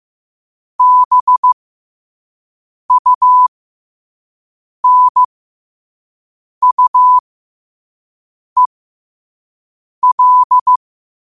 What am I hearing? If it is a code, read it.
Morse code, heard as a keyed tone, about 15 wpm, spaced as for 5 wpm, 1000 Hz. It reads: BUNUEL
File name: qotdfiles / 5wpm